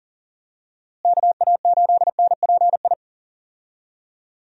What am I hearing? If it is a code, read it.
Morse code, heard as a keyed tone, about 40 wpm, 700 Hz. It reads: KA8NPI
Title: USA Call Signs - 6 Characters - Repeated 2x - 40wpm